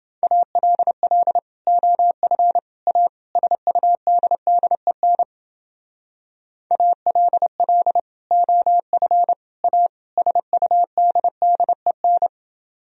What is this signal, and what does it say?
Morse code, heard as a keyed tone, about 30 wpm, 700 Hz. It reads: ALL OF A SUDDEN ALL OF A SUDDEN